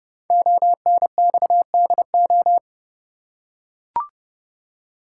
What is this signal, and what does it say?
Morse code, heard as a keyed tone, about 30 wpm, 700 Hz. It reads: ONXDO
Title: Code Groups - 5 Letters - 30wpm